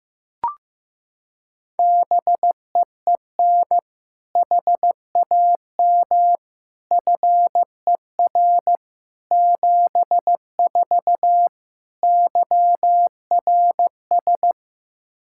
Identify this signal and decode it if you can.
Morse code, heard as a keyed tone, about 16 wpm, 700 Hz. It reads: BEEN HAM FER 74 YRS
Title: Practice QSO - Exchange 3 - More Info - 15wpm